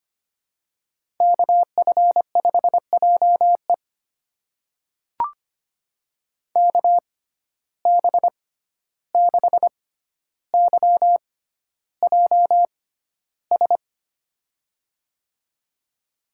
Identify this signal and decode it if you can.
Morse code, heard as a keyed tone, about 25 wpm, spaced as for 12 wpm, 700 Hz. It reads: KF5JE KB6YJS
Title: USA Call Signs - 6x Character Spacing - 25wpm